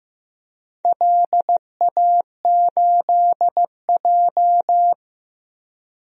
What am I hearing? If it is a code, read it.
Morse code, heard as a keyed tone, about 15 wpm, 700 Hz. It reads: LA8J
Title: DXCC-100 Call Signs - 4 Characters - 15wpm